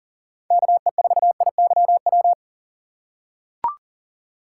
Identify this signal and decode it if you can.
Morse code, heard as a keyed tone, about 37 wpm, 700 Hz. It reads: KE4IYW
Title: USA Call Signs - 6 Characters - 4x Character Spacing - 40wpm